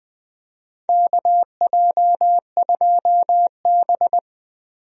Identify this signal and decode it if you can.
Morse code, heard as a keyed tone, about 20 wpm, 700 Hz. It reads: KJ2B
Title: USA Call Signs - 4 Characters - 20wpm